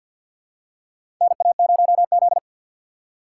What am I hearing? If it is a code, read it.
Morse code, heard as a keyed tone, about 52 wpm, 700 Hz. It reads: NA0Z
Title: USA Call Signs - 4 Characters - 6x Character Spacing - 50wpm